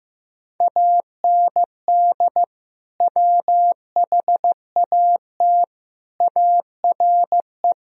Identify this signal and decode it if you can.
Morse code, heard as a keyed tone, about 16 wpm, 700 Hz. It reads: AND WHAT ARE
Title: Sentences from Top 1000 Words - Review Part 2 - 15wpm